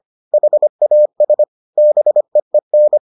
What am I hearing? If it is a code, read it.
Morse code, heard as a keyed tone, about 25 wpm, 600 Hz. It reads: HAS BEEN